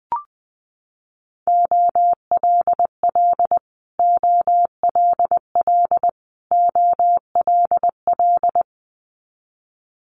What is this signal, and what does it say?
Morse code, heard as a keyed tone, about 20 wpm, 700 Hz. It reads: OLL OLL OLL
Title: Multi-letter Phonemes Repeated 3x 20wpm